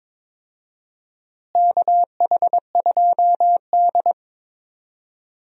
Morse code, heard as a keyed tone, about 21 wpm, 700 Hz. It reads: KH2D
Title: USA Call Signs - 4 Characters - Repeated 2x - 22wpm